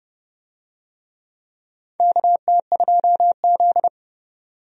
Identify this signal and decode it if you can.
Morse code, heard as a keyed tone, about 29 wpm, 700 Hz. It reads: KT2Z
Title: USA Call Signs - 4 Characters - 30wpm